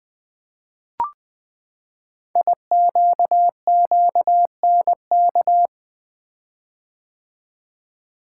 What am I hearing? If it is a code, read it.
Morse code, heard as a keyed tone, about 20 wpm, 700 Hz. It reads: IQQNK
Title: Code Groups - 5 Letters - 20wpm